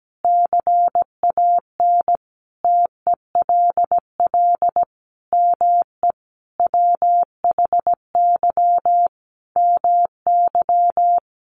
Morse code, heard as a keyed tone, about 17 wpm, 700 Hz. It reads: CAN TELL ME WHY MY
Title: Sentences from Top 1200 Words 17wpm